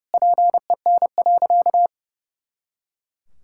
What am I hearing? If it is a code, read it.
Morse code, heard as a keyed tone, about 32 wpm, 700 Hz. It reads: PEN.
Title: Pangram Challenge v2 - 30wpm